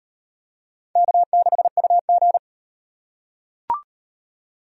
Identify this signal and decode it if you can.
Morse code, heard as a keyed tone, about 35 wpm, 700 Hz. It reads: K6UG